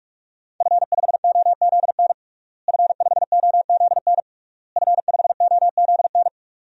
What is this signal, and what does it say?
Morse code, heard as a keyed tone, about 46 wpm, 700 Hz. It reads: F5OZN F5OZN F5OZN
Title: DXCC-100 Call Signs - 5 Characters - Repeated 3x - 45wpm